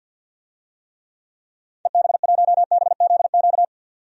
Morse code, heard as a keyed tone, about 51 wpm, 700 Hz. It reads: EB1BZQ
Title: DXCC-100 Call Signs - 6 Characters - 50wpm